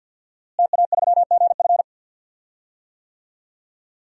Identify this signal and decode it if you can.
Morse code, heard as a keyed tone, about 53 wpm, 700 Hz. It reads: TA3GF
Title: DXCC-100 Call Signs - 5 Characters - Repeated 3x - 50wpm